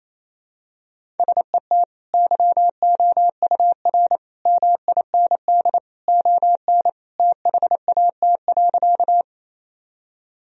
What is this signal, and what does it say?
Morse code, heard as a keyed tone, about 28 wpm, 700 Hz. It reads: SET YOUR MIND ON THAT.
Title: Sentences from Top 1000 Words - Review Part 1 - 28wpm